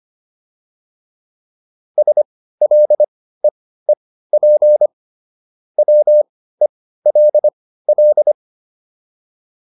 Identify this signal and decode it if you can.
Morse code, heard as a keyed tone, about 25 wpm, spaced as for 15 wpm, 600 Hz. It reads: SLEEP WELL